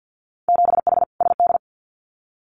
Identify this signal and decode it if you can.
Morse code, heard as a keyed tone, about 50 wpm, 700 Hz. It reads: 7H SD
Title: Winter Field Day Exchanges Encore 50wpm